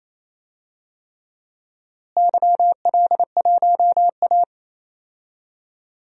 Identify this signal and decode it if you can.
Morse code, heard as a keyed tone, about 28 wpm, 700 Hz. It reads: YL1A
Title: DXCC-100 Call Signs - 4 Characters - Repeated 3x - 28wpm